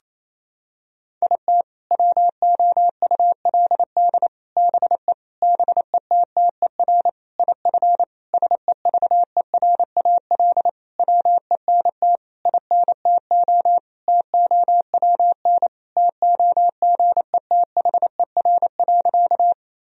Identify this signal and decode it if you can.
Morse code, heard as a keyed tone, about 27 wpm, 700 Hz. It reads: IT WOULD BE BETTER IF SEVERAL WENT INTO TOWN TOGETHER.
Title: Sentences from Top 1000 Words - Review Part 2 - 28wpm